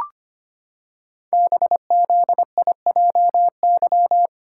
Morse code, heard as a keyed tone, about 25 wpm, 700 Hz. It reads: BZIJY
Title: Code Groups - 5 Letters - 25wpm